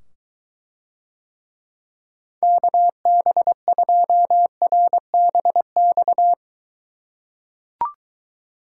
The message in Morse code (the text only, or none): KB2RBX